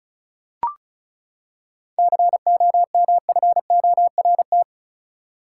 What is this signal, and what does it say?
Morse code, heard as a keyed tone, about 34 wpm, 700 Hz. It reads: COMFORT
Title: Seven Letter Words - 35wpm